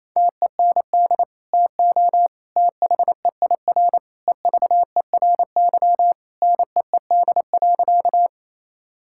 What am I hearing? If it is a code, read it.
Morse code, heard as a keyed tone, about 27 wpm, 700 Hz. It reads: TEND TO THEIR EVERY NEED.
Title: Sentences from Top 1000 Words - Review Part 5 - 28wpm